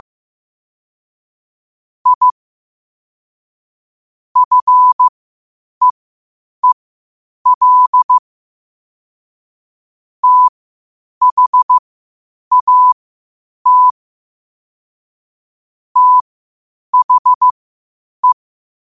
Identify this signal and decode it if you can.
Morse code, heard as a keyed tone, about 15 wpm, spaced as for 8 wpm, 1000 Hz. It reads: I FEEL THAT THE
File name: qotdfiles / 8wpm